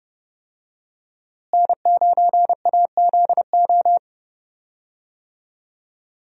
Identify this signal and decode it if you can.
Morse code, heard as a keyed tone, about 30 wpm, 700 Hz. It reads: N9AZO
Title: USA Call Signs - 5 Characters - Repeated 2x - 30wpm